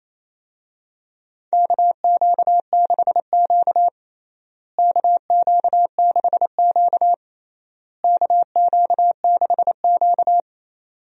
Morse code, heard as a keyed tone, about 28 wpm, 700 Hz. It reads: KQ6Q KQ6Q KQ6Q
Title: USA Call Signs - 4 Characters - Repeated 3x - 28wpm